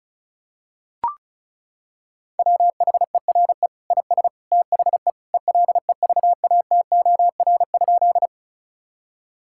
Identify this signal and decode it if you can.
Morse code, heard as a keyed tone, about 34 wpm, 700 Hz. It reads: WHERE IS THE ELEVATOR?
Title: Sentences from Top 1600 Words 35wpm